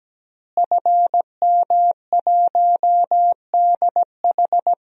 Morse code, heard as a keyed tone, about 17 wpm, 700 Hz. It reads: FM1DH
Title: DXCC-100 Call Signs - 5 Characters - Repeated 2x - 17wpm